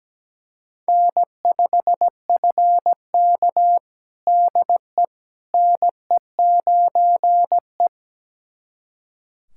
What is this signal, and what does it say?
Morse code, heard as a keyed tone, about 17 wpm, 700 Hz. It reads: N5FK DE NE9E
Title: USA Callsign DE Callsign - 4 Characters - 17wpm